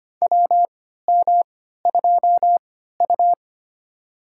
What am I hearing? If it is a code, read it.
Morse code, heard as a keyed tone, about 25 wpm, spaced as for 14 wpm, 700 Hz. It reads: WM2U
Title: USA Call Signs - 3x Character Spacing - 25wpm